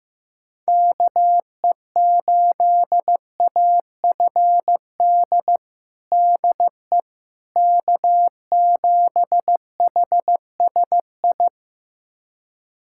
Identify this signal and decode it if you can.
Morse code, heard as a keyed tone, about 16 wpm, 700 Hz. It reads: KE8AFD DE K7HSI
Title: USA Callsign DE Callsign 15wpm